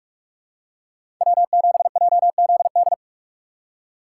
Morse code, heard as a keyed tone, about 46 wpm, 700 Hz. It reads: W7JZD
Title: USA Call Signs - 5 Characters - 3x Character Spacing - 45wpm